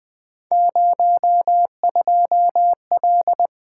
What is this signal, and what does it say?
Morse code, heard as a keyed tone, about 20 wpm, 700 Hz. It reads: 02L
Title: Three Letter-Number Combinations - 20wpm